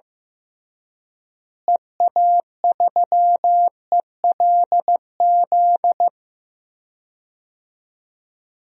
Morse code, heard as a keyed tone, about 16 wpm, 700 Hz. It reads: EA3ELZ
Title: DXCC-100 Call Signs - 6 Characters - Repeated 2x - 15wpm